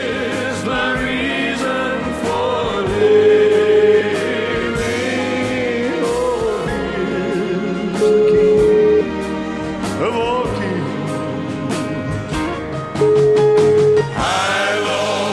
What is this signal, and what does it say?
Ringback tone (Europe, 425 Hz), with noise.